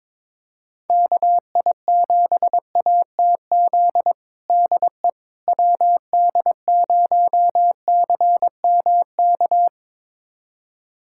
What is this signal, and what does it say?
Morse code, heard as a keyed tone, about 23 wpm, 700 Hz. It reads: KI7ATZ DE WD0CMK
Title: USA Callsign DE Callsign - 6 Characters - 22wpm